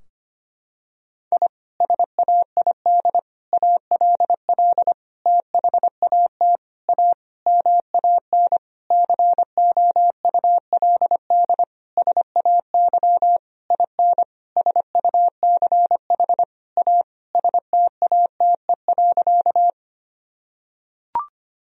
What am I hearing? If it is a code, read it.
Morse code, heard as a keyed tone, about 25 wpm, 700 Hz. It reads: I SAID ALL THAT A MAN COULD SAY IN SUCH A STATE.
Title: Sentences from Top 1000 Words - Review Part 2 - 25wpm